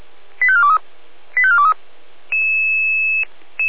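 Five-tone selective call (ZVEI), heard as quick runs of five tones, tones of about 70 ms. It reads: 85423 85423
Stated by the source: Five-tone ZVEI selective call 85250, received by SDR, from the feuerwehr-orlamuende.org fire brigade audio archive.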